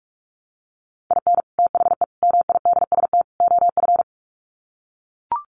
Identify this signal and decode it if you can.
Morse code, heard as a keyed tone, about 44 wpm, 700 Hz. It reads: IN THE MIDST OF
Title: Sets of 4 Words 45wpm